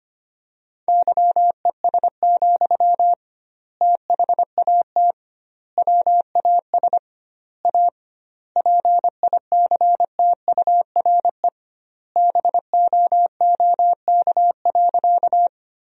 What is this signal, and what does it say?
Morse code, heard as a keyed tone, about 25 wpm, 700 Hz. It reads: YES, THAT WAS A PICTURE BOOK.
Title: 2x Word-Spacing - Sentences from Top 500 Words - 25wpm